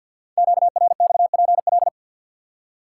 Morse code, heard as a keyed tone, about 48 wpm, 700 Hz. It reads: QRXPL
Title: Code Groups - 5 Letters - 50wpm